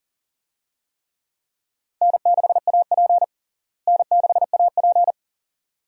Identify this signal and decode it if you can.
Morse code, heard as a keyed tone, about 40 wpm, 700 Hz. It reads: N6AP N6AP
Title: USA Call Signs - 4 Characters - Repeated 2x - 40wpm